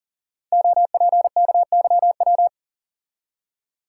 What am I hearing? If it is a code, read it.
Morse code, heard as a keyed tone, about 42 wpm, 690 Hz. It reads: OPKYW
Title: Code Groups - 5 Letter:Numbers - 40wpm